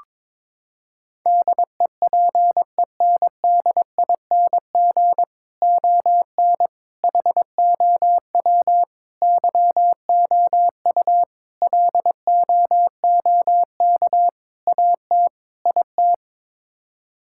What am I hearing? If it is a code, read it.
Morse code, heard as a keyed tone, about 22 wpm, 700 Hz. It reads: DEPENDING ON HOW YOU LOOK AT IT
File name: practice / 22wpm / Sets of 7 Words - 22wpm